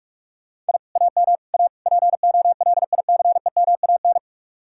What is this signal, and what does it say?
Morse code, heard as a keyed tone, about 63 wpm, spaced as for 54 wpm, 700 Hz. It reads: I AM A POLICEMAN
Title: Speed-Racing Common QSO Phrases - 45wpm